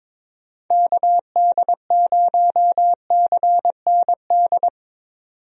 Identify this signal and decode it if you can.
Morse code, heard as a keyed tone, about 22 wpm, 700 Hz. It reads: KD0CND
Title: USA Call Signs - 6 Characters - 22wpm